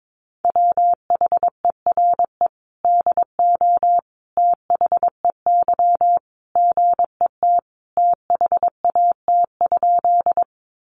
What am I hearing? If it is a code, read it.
Morse code, heard as a keyed tone, about 22 wpm, 700 Hz. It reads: WHERE DO THEY GET THAT?
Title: Sentences from Top 200 Words 22wpm